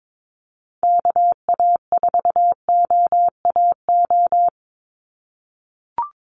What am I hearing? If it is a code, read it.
Morse code, heard as a keyed tone, about 22 wpm, 700 Hz. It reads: KA4OAO
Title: USA Call Signs 22wpm